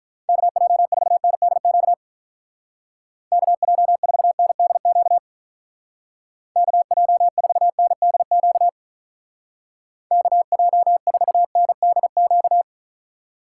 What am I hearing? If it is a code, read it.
Morse code, heard as a keyed tone, about 45 wpm, 700 Hz. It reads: KJ4NDQ KJ4NDQ KJ4NDQ KJ4NDQ